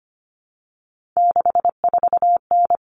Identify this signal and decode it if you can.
Morse code, heard as a keyed tone, about 25 wpm, 700 Hz. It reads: 64N
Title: All Cut Numbers - 3-Digit 25wpm